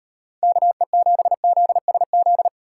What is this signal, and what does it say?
Morse code, heard as a keyed tone, about 38 wpm, 700 Hz. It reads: KE7ZSZ